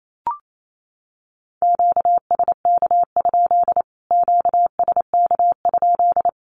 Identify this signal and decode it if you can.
Morse code, heard as a keyed tone, about 28 wpm, 700 Hz. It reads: QSK? QSK?
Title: Q-Codes Repeated 3x 28wpm